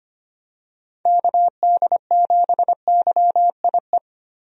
Morse code, heard as a keyed tone, about 25 wpm, 700 Hz. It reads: KD7YIE
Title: USA Call Signs - 6 Characters - 25wpm